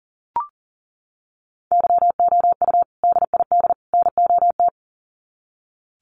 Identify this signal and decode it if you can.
Morse code, heard as a keyed tone, about 40 wpm, 700 Hz. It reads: YOU DID NOT
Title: Sets of 3 Words 40wpm